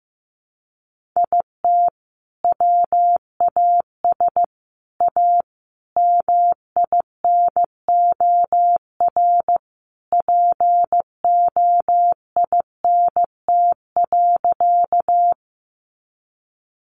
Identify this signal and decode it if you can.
Morse code, heard as a keyed tone, about 15 wpm, 700 Hz. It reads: IT WAS A MINOR POINT.
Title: Sentences from Top 1500 Words 15wpm